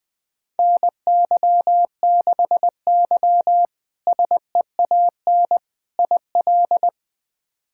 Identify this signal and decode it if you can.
Morse code, heard as a keyed tone, about 20 wpm, 700 Hz. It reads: NY6Y SEAN IL
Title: K1USN SST QSO Practice - 20wpm